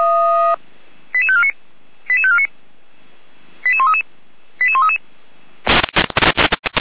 Five-tone selective call (ZVEI), heard as quick runs of five tones, tones of about 70 ms. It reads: 88549 88549 88133 88133
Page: /455/SDR/ZVEI/